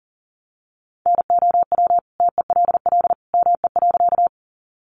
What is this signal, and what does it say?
Morse code, heard as a keyed tone, about 37 wpm, 700 Hz. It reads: NOW TELL ME.
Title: Sentences from Top 200 Words 40wpm